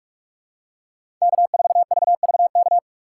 Morse code, heard as a keyed tone, about 46 wpm, 700 Hz. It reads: K4VVK
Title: USA Call Signs - 5 Characters - 2x Character Spacing - 45wpm